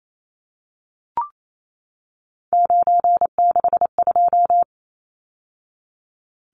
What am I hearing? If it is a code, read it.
Morse code, heard as a keyed tone, about 28 wpm, 700 Hz. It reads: 962
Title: Numbers - 3-Digit 28wpm